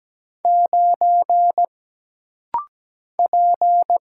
Morse code, heard as a keyed tone, about 17 wpm, 700 Hz. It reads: 9 P
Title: Single Letter-Number Alt - Rapid-Fire 17wpm